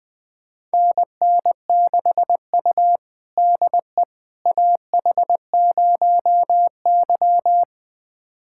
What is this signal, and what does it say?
Morse code, heard as a keyed tone, about 20 wpm, 700 Hz. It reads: NN6U DE AH0Y
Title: USA Callsign DE Callsign - 4 Characters - 20wpm